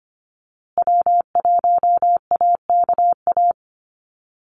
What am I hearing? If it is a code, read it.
Morse code, heard as a keyed tone, about 25 wpm, 700 Hz. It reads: W1AKA
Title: USA Call Signs Repeated 3x 25wpm